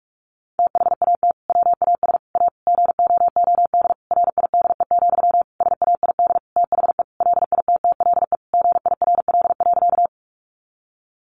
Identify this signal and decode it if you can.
Morse code, heard as a keyed tone, about 44 wpm, 700 Hz. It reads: THAT WAS A GOOD RIDE, SAID THE LITTLE GIRL.
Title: Sentences from Top 1200 Words 45wpm